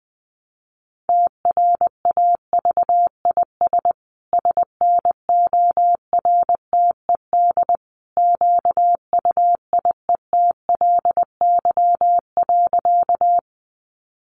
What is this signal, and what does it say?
Morse code, heard as a keyed tone, about 20 wpm, 700 Hz. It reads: TRAVIS SNORTED QUIETLY.
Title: A Sound Of Thunder 20wpm